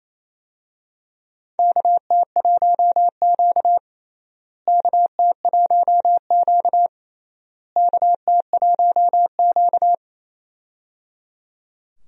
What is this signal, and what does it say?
Morse code, heard as a keyed tone, about 27 wpm, 700 Hz. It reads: KT1Q KT1Q KT1Q
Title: USA Call Signs - 4 Characters - Repeated 3x - 28wpm